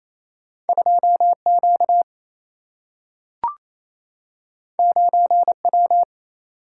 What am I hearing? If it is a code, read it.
Morse code, heard as a keyed tone, about 28 wpm, 700 Hz. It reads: 2 Q 9 W
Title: Single-Letter Single-Number 28wpm